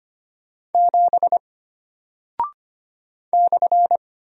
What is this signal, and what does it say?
Morse code, heard as a keyed tone, about 25 wpm, 700 Hz. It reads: 7 /
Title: Single Character (Letter:Number:Punctuation:Prosign) Alt 25wpm